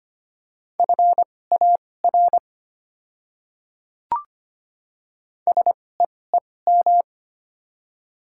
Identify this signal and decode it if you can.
Morse code, heard as a keyed tone, about 25 wpm, spaced as for 18 wpm, 700 Hz. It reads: FAR SEEM
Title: 2x Character Spacing - Top 300 Words - 25wpm